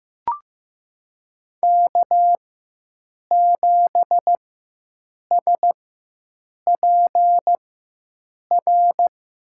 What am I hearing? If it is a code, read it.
Morse code, heard as a keyed tone, about 15 wpm, spaced as for 7 wpm, 700 Hz. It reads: K7SPR
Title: USA Call Signs - 4x Character Spacing - 15wpm